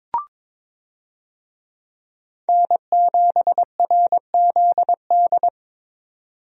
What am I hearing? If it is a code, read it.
Morse code, heard as a keyed tone, about 21 wpm, 700 Hz. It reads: N7RZD
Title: USA Call Signs - 5 Characters - 22wpm